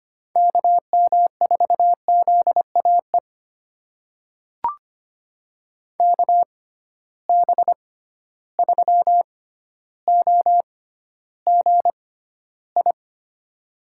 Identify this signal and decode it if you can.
Morse code, heard as a keyed tone, about 25 wpm, spaced as for 13 wpm, 700 Hz. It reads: KM4ZAE KB3OGI
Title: USA Call Signs - 6 Characters - 6x Character Spacing - 25wpm